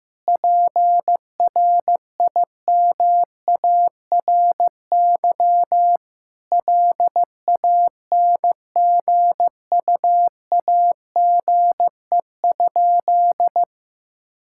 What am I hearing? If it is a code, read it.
Morse code, heard as a keyed tone, about 15 wpm, 700 Hz. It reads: PRIMARY LANGUAGE?
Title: Sentences from Top 1000 Words - Review Part 5 - 15wpm